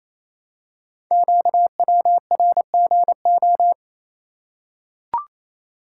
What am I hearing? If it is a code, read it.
Morse code, heard as a keyed tone, about 29 wpm, 700 Hz. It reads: QWRGO
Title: Code Groups - 5 Letters - 28wpm